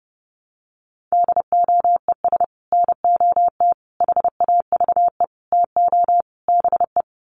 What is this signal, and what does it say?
Morse code, heard as a keyed tone, about 30 wpm, 700 Hz. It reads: DOES NOT HAVE TO BE
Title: Sets of 5 Words 30wpm